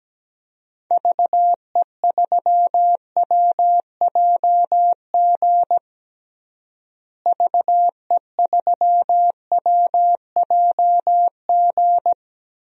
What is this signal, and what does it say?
Morse code, heard as a keyed tone, about 17 wpm, 700 Hz. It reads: VE3WJG VE3WJG
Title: DXCC-100 Call Signs - 6 Characters - Repeated 2x - 17wpm